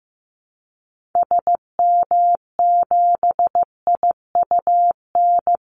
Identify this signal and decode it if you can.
Morse code, heard as a keyed tone, about 15 wpm, 700 Hz. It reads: SM7IUN
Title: Call Signs Repeated 3x - 15wpm